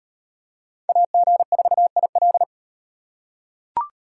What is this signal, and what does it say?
Morse code, heard as a keyed tone, about 38 wpm, 700 Hz. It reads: AG4IL